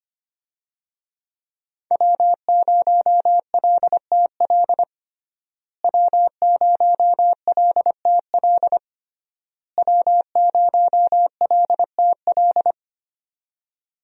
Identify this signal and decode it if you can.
Morse code, heard as a keyed tone, about 25 wpm, 700 Hz. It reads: W0LTL W0LTL W0LTL
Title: USA Call Signs - 5 Characters - Repeated 3x - 25wpm